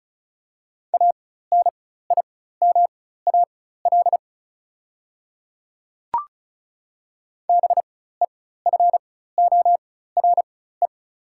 Morse code, heard as a keyed tone, about 36 wpm, spaced as for 16 wpm, 700 Hz. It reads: ANIMAL BEFORE
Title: 4x Character Spacing - Top 200 Words - 35wpm